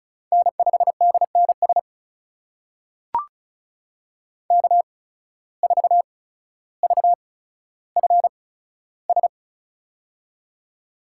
Morse code, heard as a keyed tone, about 36 wpm, spaced as for 15 wpm, 700 Hz. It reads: N5DNS K4VFS
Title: USA Call Signs - 5 Characters - 8x Character Spacing - 35wpm